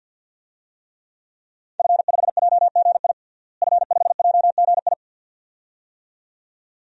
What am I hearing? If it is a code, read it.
Morse code, heard as a keyed tone, about 52 wpm, 700 Hz. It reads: F5JGI F5JGI
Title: DXCC-100 Call Signs - 5 Characters - Repeated 2x - 50wpm